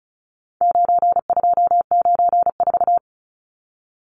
Morse code, heard as a keyed tone, about 34 wpm, 700 Hz. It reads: 9294
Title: Numbers - 4-Digit 35wpm